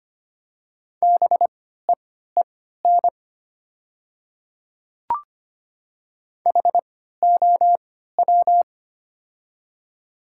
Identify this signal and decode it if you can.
Morse code, heard as a keyed tone, about 25 wpm, spaced as for 14 wpm, 690 Hz. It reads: BEEN HOW
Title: 3x Character Spacing - Top 300 Words - 25wpm